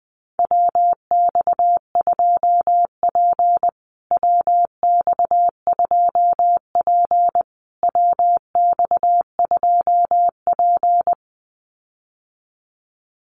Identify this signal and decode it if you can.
Morse code, heard as a keyed tone, about 20 wpm, 700 Hz. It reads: WX2P WX2P WX2P
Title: USA Call Signs Repeated 3x 20wpm